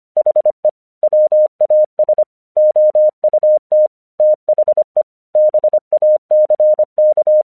Morse code, heard as a keyed tone, about 24 wpm, 610 Hz. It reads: HE WAS OUT THE BACK
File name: 1 English 25wpm / AQC17